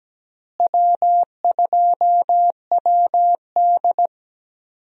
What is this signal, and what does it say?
Morse code, heard as a keyed tone, about 17 wpm, 700 Hz. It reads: W2WD
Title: USA Call Signs - 4 Characters - Repeated 2x - 17wpm